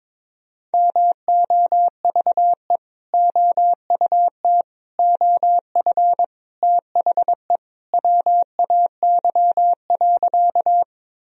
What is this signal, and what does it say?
Morse code, heard as a keyed tone, about 21 wpm, 700 Hz. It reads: MOVE OUT OF THE WAY.
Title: Sentences from Top 1000 Words - Review Part 1 - 22wpm